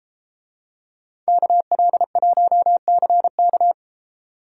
Morse code, heard as a keyed tone, about 33 wpm, 700 Hz. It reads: KL1CK